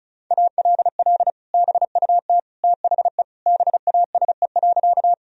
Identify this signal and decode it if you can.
Morse code, heard as a keyed tone, about 36 wpm, 700 Hz. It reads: ALL BUT THE BASE.
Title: Sentences from Top 1000 Words - Review Part 1 - 35wpm